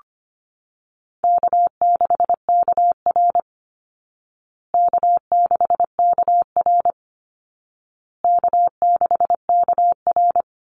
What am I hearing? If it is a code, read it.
Morse code, heard as a keyed tone, about 25 wpm, 700 Hz. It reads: K6KR K6KR K6KR
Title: Easy Call Signs Repeated 3x - 25wpm